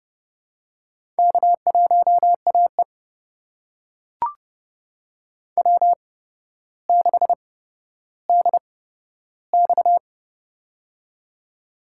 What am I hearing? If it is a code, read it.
Morse code, heard as a keyed tone, about 31 wpm, spaced as for 13 wpm, 700 Hz. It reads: K1AE W6DX
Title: USA Call Signs - 4 Characters - 8x Character Spacing - 30wpm